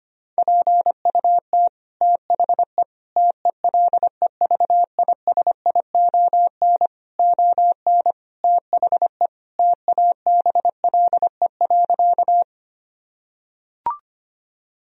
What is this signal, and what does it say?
Morse code, heard as a keyed tone, about 25 wpm, 700 Hz. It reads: PUT THE TELEVISION ON THE TABLE.
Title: Sentences from Top 1000 Words - Review Part 4 - 25wpm